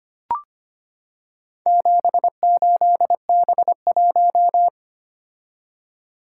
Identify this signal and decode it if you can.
Morse code, heard as a keyed tone, about 25 wpm, 700 Hz. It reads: 78B1
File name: practice / 25wpm / Four Letter-Number Combinations - 25wpm